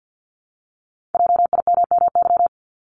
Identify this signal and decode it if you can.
Morse code, heard as a keyed tone, about 50 wpm, 690 Hz. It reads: JIMMY
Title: Multi-letter Phonemes Repeated 3x 50wpm